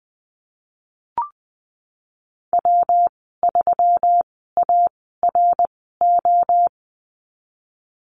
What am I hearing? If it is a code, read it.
Morse code, heard as a keyed tone, about 19 wpm, spaced as for 14 wpm, 700 Hz. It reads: W3ARO
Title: 140 Lesson 23g - Letter f - Call Signs 20wpm